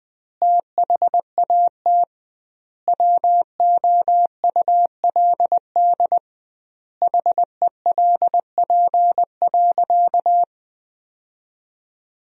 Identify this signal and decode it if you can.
Morse code, heard as a keyed tone, about 20 wpm, 700 Hz. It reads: THAT WOULD HELP.
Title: 2x Word-Spacing - Sentences from Top 200 Words - 20wpm